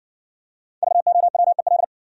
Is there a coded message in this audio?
Yes: Morse code from 0.8 to 1.8 s, heard as a keyed tone, about 60 wpm, 700 Hz. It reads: VOWEL